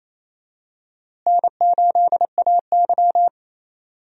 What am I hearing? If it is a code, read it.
Morse code, heard as a keyed tone, about 27 wpm, 700 Hz. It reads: N8AY
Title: USA Call Signs - 4 Characters - Repeated 3x - 28wpm